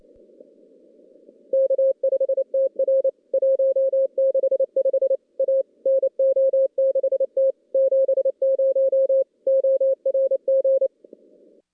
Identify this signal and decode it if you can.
Morse code, heard as a keyed tone, about 28 wpm, 540 Hz. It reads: K5TR 165 A NO6T 70 ORG
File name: contests / CW-SS-k5tr- -2300